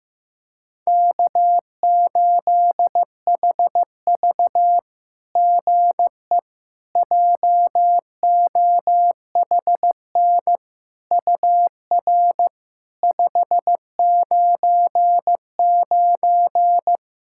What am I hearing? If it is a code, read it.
Morse code, heard as a keyed tone, about 15 wpm, 700 Hz. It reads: K8HV GE JOHN UR 599
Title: SOTA QSO Practice 15wpm